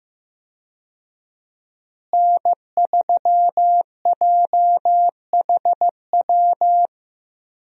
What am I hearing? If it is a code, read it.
Morse code, heard as a keyed tone, about 15 wpm, 700 Hz. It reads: N3JHW
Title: USA Call Signs - 5 Characters - Repeated 2x - 15wpm